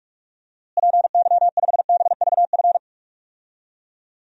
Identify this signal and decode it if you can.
Morse code, heard as a keyed tone, about 46 wpm, 700 Hz. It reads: PY5BVF
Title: DXCC-100 Call Signs - 6 Characters - Repeated 3x - 45wpm